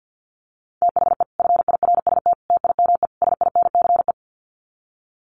Morse code, heard as a keyed tone, about 50 wpm, 700 Hz. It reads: THE FIRST TIME SINCE
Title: Sets of 4 Words 50wpm